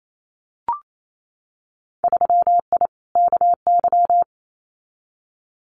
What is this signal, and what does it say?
Morse code, heard as a keyed tone, about 28 wpm, 700 Hz. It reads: 3I KY
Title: Winter Field Day Exchanges 28wpm